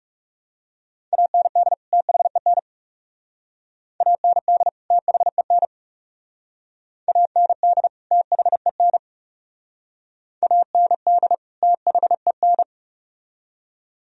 Morse code, heard as a keyed tone, about 36 wpm, 700 Hz. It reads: AND THEN AND THEN AND THEN AND THEN
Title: Speed-Racing - Sets of 2 Words 30wpm